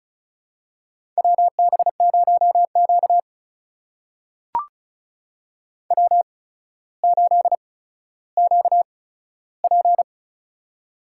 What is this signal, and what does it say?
Morse code, heard as a keyed tone, about 36 wpm, spaced as for 15 wpm, 700 Hz. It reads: WB0Q W8QP
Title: USA Call Signs - 4 Characters - 8x Character Spacing - 35wpm